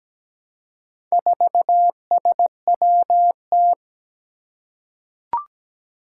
Morse code, heard as a keyed tone, about 17 wpm, 700 Hz. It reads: 4SWT
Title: Four Challenging Letter-Number Combinations 17wpm